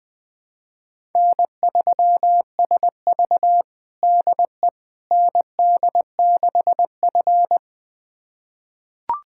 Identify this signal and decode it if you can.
Morse code, heard as a keyed tone, about 20 wpm, 700 Hz. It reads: N3SV DE ND6F
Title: USA Callsign DE Callsign - 4 Characters - 20wpm